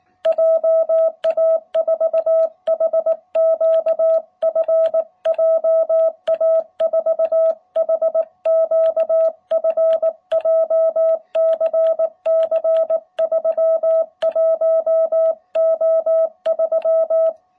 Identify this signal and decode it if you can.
Morse code, heard as a keyed tone, about 19 wpm, 660 Hz. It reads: JA4HQF JA4HQF JCC 31O3